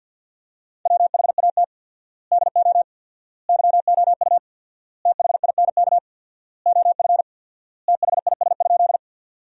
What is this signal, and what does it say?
Morse code, heard as a keyed tone, about 53 wpm, 700 Hz. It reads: WHAT DO YOU THINK OF THIS?
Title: 4x Word-Spacing - Sentences from Top 200 Words - 50wpm